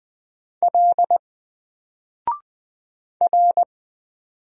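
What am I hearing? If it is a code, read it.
Morse code, heard as a keyed tone, about 20 wpm, 700 Hz. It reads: L R
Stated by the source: Single letter and number Morse drill at 20 wpm, from N4LCD.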